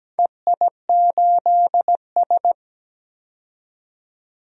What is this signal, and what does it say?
Morse code, heard as a keyed tone, about 17 wpm, 700 Hz. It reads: EI8S
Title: DXCC-100 Call Signs - 4 Characters - 17wpm